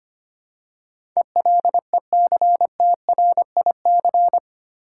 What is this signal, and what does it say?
Morse code, heard as a keyed tone, about 25 wpm, 700 Hz. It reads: ELECTRIC
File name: 8 Letter Words - First 3 Letters Given - 25wpm